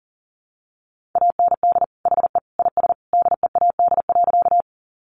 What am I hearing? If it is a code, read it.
Morse code, heard as a keyed tone, about 40 wpm, 700 Hz. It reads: AND HE IS DEAD.
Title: Sentences from Top 1500 Words 40wpm